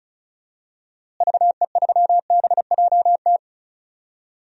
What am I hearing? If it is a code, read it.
Morse code, heard as a keyed tone, about 36 wpm, 700 Hz. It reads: VE3BJT
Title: DXCC-100 Call Signs - 6 Characters - Repeated 3x - 35wpm